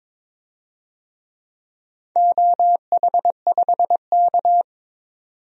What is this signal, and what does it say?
Morse code, heard as a keyed tone, about 22 wpm, 700 Hz. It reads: OH5K
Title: DXCC-100 Call Signs - 4 Characters - 22wpm